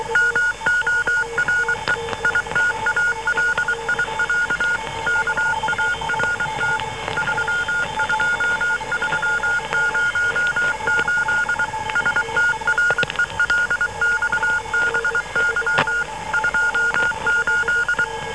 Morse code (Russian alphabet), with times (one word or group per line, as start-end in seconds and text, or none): 0.2-1.9 МОРЕ
2.2-4.8 ИТАЛИЯ
5.1-6.8 КАРТ
7.2-11.6 32308
12.0-16.0 СТРЕЛЬБЫ
16.3-18.0 28